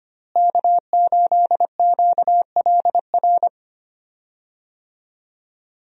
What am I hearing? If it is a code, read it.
Morse code, heard as a keyed tone, about 25 wpm, 700 Hz. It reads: K8QLR
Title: USA Call Signs - 5 Characters - Repeated 3x - 25wpm